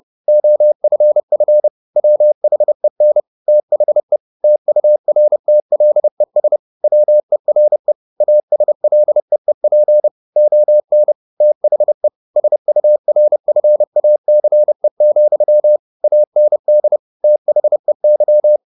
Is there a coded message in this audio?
Yes: Morse code from 0.3 to 18.6 s, heard as a keyed tone, about 30 wpm, 600 Hz. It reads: OFF WHEN THE TURTLES WERE ASLEEP ON THE SURFACE, AND THEY